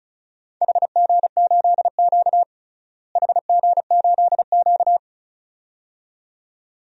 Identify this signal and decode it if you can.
Morse code, heard as a keyed tone, about 36 wpm, 700 Hz. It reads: HG8Q HG8Q
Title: DXCC-100 Call Signs - 4 Characters - Repeated 2x - 35wpm